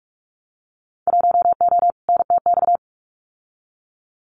Morse code, heard as a keyed tone, about 46 wpm, 700 Hz. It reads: 1O NTX
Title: Winter Field Day Exchanges 45wpm